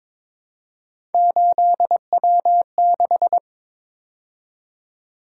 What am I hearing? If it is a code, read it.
Morse code, heard as a keyed tone, about 26 wpm, 700 Hz. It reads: 8W6
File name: Speed-Racing - Three Letter-Number Combinations - 22wpm